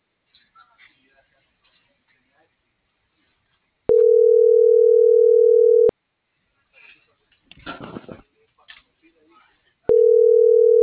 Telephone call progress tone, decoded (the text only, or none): ringback tone